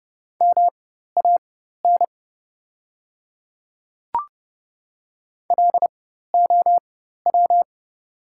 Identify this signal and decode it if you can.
Morse code, heard as a keyed tone, about 30 wpm, spaced as for 14 wpm, 690 Hz. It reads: MAN LOW
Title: 4x Character Spacing - Top 200 Words - 30wpm